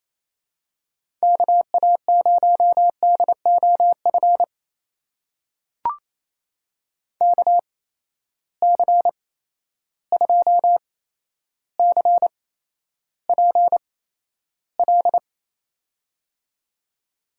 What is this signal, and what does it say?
Morse code, heard as a keyed tone, about 28 wpm, spaced as for 12 wpm, 700 Hz. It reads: KA0DOF KC2CPL
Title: USA Call Signs - 6 Characters - 8x Character Spacing - 28wpm